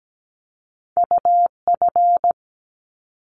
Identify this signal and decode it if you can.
Morse code, heard as a keyed tone, about 17 wpm, 700 Hz. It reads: UF
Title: Top Two Letter Combinations in Words 17wpm